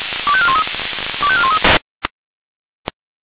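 Five-tone selective call (ZVEI), heard as quick runs of five tones, tones of about 70 ms. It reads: 25413 25413
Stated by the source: ZVEI five-tone selective call received on SDR.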